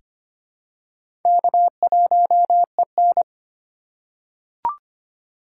Morse code, heard as a keyed tone, about 25 wpm, 700 Hz. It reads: K1EN
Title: USA Call Signs - 4 Characters - 4x Character Spacing - 25wpm